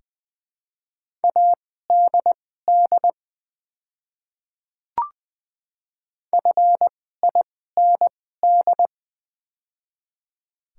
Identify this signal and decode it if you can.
Morse code, heard as a keyed tone, about 20 wpm, spaced as for 15 wpm, 700 Hz. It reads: ADD FIND
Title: 2x Character Spacing - Top 200 Words - 20wpm